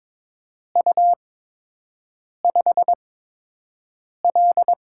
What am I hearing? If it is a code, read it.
Morse code, heard as a keyed tone, about 22 wpm, spaced as for 6 wpm, 700 Hz. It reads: U5L
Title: USA Call Signs - 4 Characters - 8x Character Spacing - 22wpm